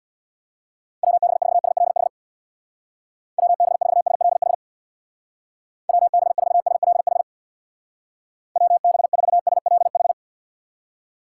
Morse code, heard as a keyed tone, about 64 wpm, 700 Hz. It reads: WB4SLH WB4SLH WB4SLH WB4SLH